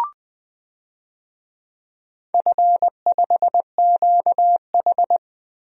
Morse code, heard as a keyed tone, about 20 wpm, 700 Hz. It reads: F5QH
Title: DXCC-100 Call Signs - 4 Characters - Repeated 3x - 20wpm